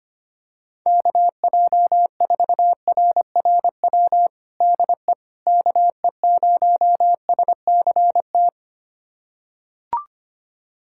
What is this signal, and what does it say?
Morse code, heard as a keyed tone, about 25 wpm, 700 Hz. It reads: KJ4RRW DE KE0SCT
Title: USA Callsign DE Callsign - 6 Characters - 25wpm